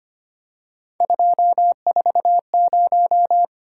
Morse code, heard as a keyed tone, about 29 wpm, 700 Hz. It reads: 240
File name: Speed-Racing - Three Letter-Number Combinations - 25wpm